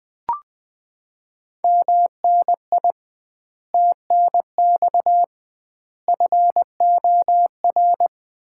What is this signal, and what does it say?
Morse code, heard as a keyed tone, about 20 wpm, 700 Hz. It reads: MNI TNX FOR
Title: Common QSO Phrases - 2x Word-Spacing - 20wpm